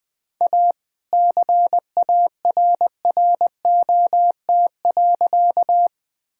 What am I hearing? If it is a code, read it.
Morse code, heard as a keyed tone, about 21 wpm, 700 Hz. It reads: A CARROT.
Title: All-Age Appropriate Jokes - 20wpm